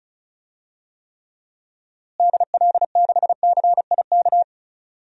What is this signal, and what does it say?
Morse code, heard as a keyed tone, about 34 wpm, 700 Hz. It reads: DL6CIK
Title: DXCC-100 Call Signs - 6 Characters - Repeated 3x - 35wpm